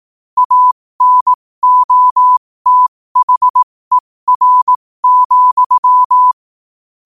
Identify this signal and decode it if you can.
Morse code, heard as a keyed tone, about 18 wpm, spaced as for 15 wpm, 1000 Hz. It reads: ANOTHER,
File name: qotdfiles / 15wpm